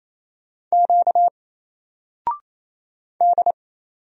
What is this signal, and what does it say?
Morse code, heard as a keyed tone, about 27 wpm, 700 Hz. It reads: Q D
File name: Single-Letter-Number-28wpm